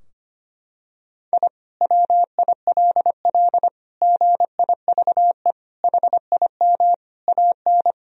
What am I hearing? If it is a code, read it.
Morse code, heard as a keyed tone, about 25 wpm, 700 Hz. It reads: I WILL GIVE HIM AN
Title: Sentences from Top 1000 Words - Review Part 4 - 25wpm